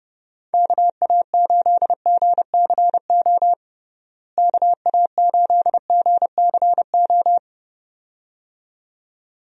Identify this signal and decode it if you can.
Morse code, heard as a keyed tone, about 30 wpm, 700 Hz. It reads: KA8GCO KA8GCO
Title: USA Call Signs - 6 Characters - Repeated 2x - 30wpm